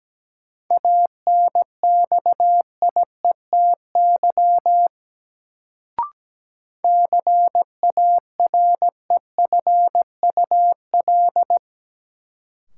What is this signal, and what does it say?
Morse code, heard as a keyed tone, about 17 wpm, 700 Hz. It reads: ANXIETY CAREFUL
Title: Sentences from Top 1600 Words - New Word Drill Alt - 17wpm